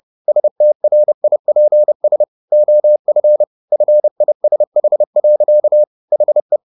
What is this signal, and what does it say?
Morse code, heard as a keyed tone, about 30 wpm, 600 Hz. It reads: STRIPS OF FISH. HE
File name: the old man and the sea / the old man and the sea 1 30wpm